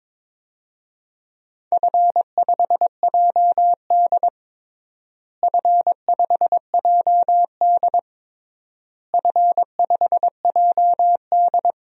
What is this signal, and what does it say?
Morse code, heard as a keyed tone, about 22 wpm, 700 Hz. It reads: F5JD F5JD F5JD
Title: DXCC-100 Call Signs - 4 Characters - Repeated 3x - 22wpm